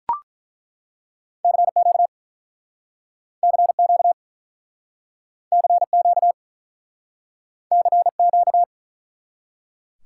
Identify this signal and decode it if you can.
Morse code, heard as a keyed tone, about 50 wpm, 700 Hz. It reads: CQ CQ CQ CQ